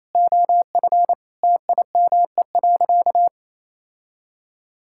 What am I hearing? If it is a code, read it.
Morse code, heard as a keyed tone, about 27 wpm, 700 Hz. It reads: OF TIME.
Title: Sentences from Top 1000 Words - Review Part 5 - 28wpm